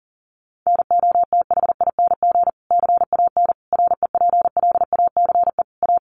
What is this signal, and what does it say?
Morse code, heard as a keyed tone, about 40 wpm, 700 Hz. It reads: NOTHING CAN REPLACE A
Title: Sentences from Top 1400 Words 40wpm